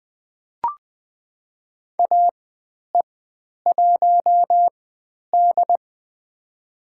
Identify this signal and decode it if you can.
Morse code, heard as a keyed tone, about 20 wpm, spaced as for 10 wpm, 700 Hz. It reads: AE1D
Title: 102 Lesson 17g - Letter u - Call Signs 20wpm